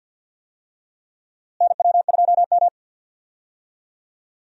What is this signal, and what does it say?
Morse code, heard as a keyed tone, about 49 wpm, 700 Hz. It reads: NW2M
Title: USA Call Signs - 4 Characters - Repeated 3x - 50wpm